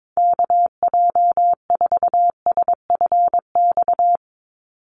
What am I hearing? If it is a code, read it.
Morse code, heard as a keyed tone, about 22 wpm, 700 Hz. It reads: KJ4SFX
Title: USA Call Signs 22wpm